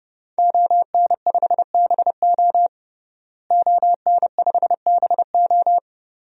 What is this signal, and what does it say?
Morse code, heard as a keyed tone, about 32 wpm, 700 Hz. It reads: ON5BO ON5BO
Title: DXCC-100 Call Signs - 5 Characters - Repeated 2x - 30wpm